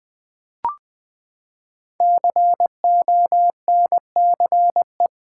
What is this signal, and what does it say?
Morse code, heard as a keyed tone, about 20 wpm, 700 Hz. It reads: CONCE
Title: Build A Word - 5 Letters At A Time - 10-Letter Words - 20wpm